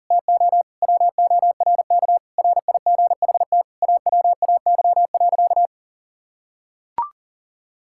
Morse code, heard as a keyed tone, about 40 wpm, 700 Hz. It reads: TO WORK RIGHT AWAY.
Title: Sentences from Top 1600 Words 40wpm